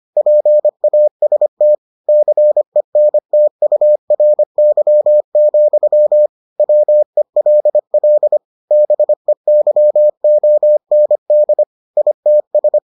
Morse code, heard as a keyed tone, about 25 wpm, 600 Hz. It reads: PAST CENTURY, WELL BEYOND ITS